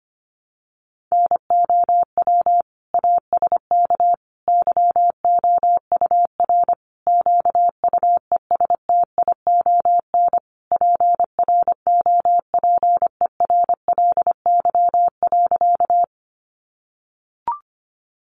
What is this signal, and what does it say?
Morse code, heard as a keyed tone, about 25 wpm, 700 Hz. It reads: NOW ASK YOUR QUESTION PROPERLY.
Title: Sentences from Top 1100 Words 25wpm